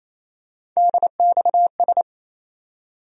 Morse code, heard as a keyed tone, about 28 wpm, 700 Hz. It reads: DXS
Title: USA Call Sign Postfixes - 3 Letters 28wpm